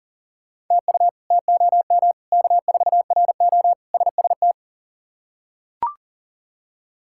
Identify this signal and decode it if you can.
Morse code, heard as a keyed tone, about 40 wpm, 700 Hz. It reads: TU TOM K4RO SST
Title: K1USN SST QSO Practice - 40wpm